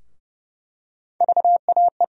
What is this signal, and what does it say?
Morse code, heard as a keyed tone, about 32 wpm, 700 Hz. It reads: VAE
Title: USA Call Sign Postfixes - 3 Letters 30wpm